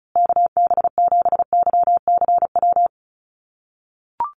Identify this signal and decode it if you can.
Morse code, heard as a keyed tone, about 34 wpm, 700 Hz. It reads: KB7YCW
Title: USA Call Signs Repeated 2x 35wpm